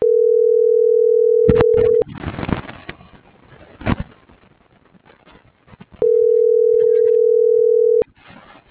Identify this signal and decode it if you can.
Ringback tone (North America), 440 + 480 Hz.